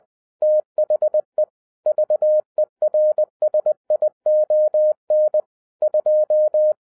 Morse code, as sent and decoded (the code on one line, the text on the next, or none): - .... / ...- . .-. ... .. --- -. / ..---
THE VERSION 2